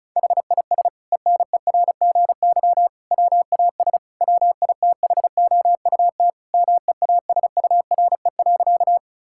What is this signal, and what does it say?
Morse code, heard as a keyed tone, about 36 wpm, 700 Hz. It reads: HIS ENERGY WAS WITHOUT MEASURE.
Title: Sentences from Top 1000 Words - Review Part 6 - 35wpm